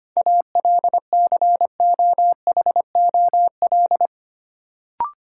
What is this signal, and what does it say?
Morse code, heard as a keyed tone, about 25 wpm, 700 Hz. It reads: ALCOHOL
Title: Sentences from Top 1400 Words - New Word Drill - 25wpm